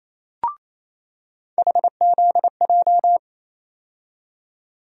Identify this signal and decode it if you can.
Morse code, heard as a keyed tone, about 28 wpm, 700 Hz. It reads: HZJ
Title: USA Call Sign Postfixes - 3 Letters 28wpm